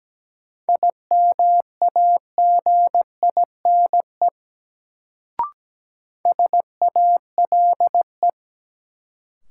Morse code, heard as a keyed tone, about 17 wpm, 700 Hz. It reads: IMAGINE SALE
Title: Sentences from Top 1400 Words - New Word Drill Alt - 17wpm